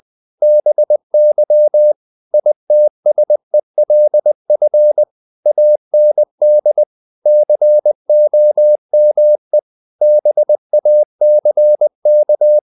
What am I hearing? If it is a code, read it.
Morse code, heard as a keyed tone, about 20 wpm, 600 Hz. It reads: BY ITSELF AND COME BACK
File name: the old man and the sea / the old man and the sea part 1 20wpm